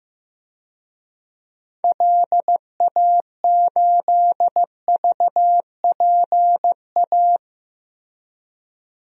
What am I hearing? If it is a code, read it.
Morse code, heard as a keyed tone, about 15 wpm, 700 Hz. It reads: LA8VPA
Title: DXCC-100 Call Signs - 6 Characters - 15wpm